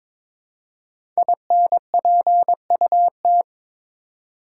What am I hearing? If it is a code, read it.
Morse code, heard as a keyed tone, about 22 wpm, 700 Hz. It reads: INPUT
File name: 5 Letter Words - First 3 Letters Given - 22wpm